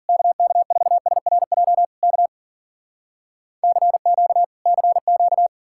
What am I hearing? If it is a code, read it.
Morse code, heard as a keyed tone, about 60 wpm, spaced as for 52 wpm, 700 Hz. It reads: KK4SRJ K CQ CQ